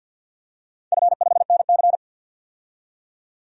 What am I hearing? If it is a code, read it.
Morse code, heard as a keyed tone, about 53 wpm, 700 Hz. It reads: F5NC
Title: DXCC-100 Call Signs - 4 Characters - 50wpm